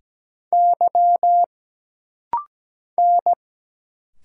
Morse code, heard as a keyed tone, about 17 wpm, 700 Hz. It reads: Y N
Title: Single Letter-Number Alt - Rapid-Fire 17wpm